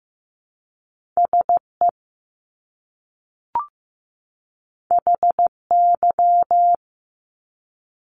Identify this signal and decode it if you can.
Morse code, heard as a keyed tone, about 15 wpm, 700 Hz. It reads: SE HY
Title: Top Two Letter Combinations in Words 15wpm